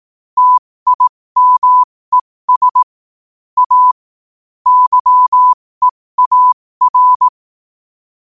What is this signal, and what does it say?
Morse code, heard as a keyed tone, about 18 wpm, spaced as for 15 wpm, 1000 Hz. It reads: TIMES A YEAR
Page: New quotes every day in morse code at 15 Words per minute.